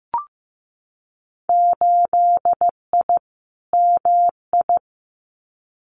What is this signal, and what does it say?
Morse code, heard as a keyed tone, about 15 wpm, 700 Hz. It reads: 8I MI
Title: Winter Field Day Exchanges Encore 15wpm